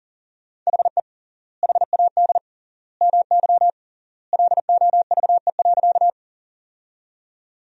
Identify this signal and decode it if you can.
Morse code, heard as a keyed tone, about 39 wpm, 700 Hz. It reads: HE HAD MY LOVE.
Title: 3x Word-Spacing - Sentences from Top 400 Words - 40wpm